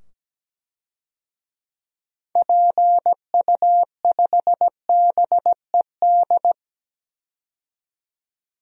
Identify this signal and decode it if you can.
Morse code, heard as a keyed tone, about 17 wpm, 700 Hz. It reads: PU5BED
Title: DXCC-100 Call Signs - 6 Characters - 17wpm